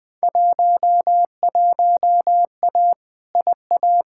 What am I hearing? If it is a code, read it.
Morse code, heard as a keyed tone, about 20 wpm, 700 Hz. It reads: 11A IA
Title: Field Day Exchanges Encore - 20wpm